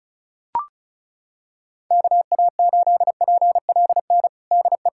Morse code, heard as a keyed tone, about 34 wpm, 700 Hz. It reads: KA8PLN DE
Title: USA Callsign DE Callsign 35wpm